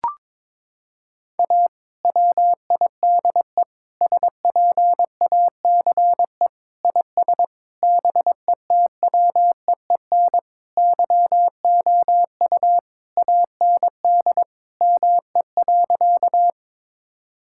Morse code, heard as a keyed tone, about 22 wpm, 700 Hz. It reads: A WIDE SPACE IS BETWEEN YOU AND ME.
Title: Sentences from Top 1000 Words - Review Part 6 - 22wpm